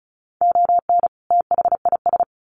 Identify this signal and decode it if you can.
Morse code, heard as a keyed tone, about 34 wpm, 700 Hz. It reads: ON THIS
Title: Sets of 2 Words 35wpm